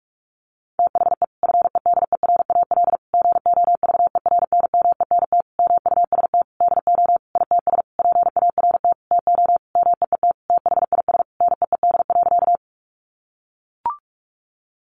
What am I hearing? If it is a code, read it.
Morse code, heard as a keyed tone, about 44 wpm, 700 Hz. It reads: THE FEDERAL GOVERNMENT MUST DO ITS PART TO MEET THIS NEED.
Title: Sentences from Top 1100 Words 45wpm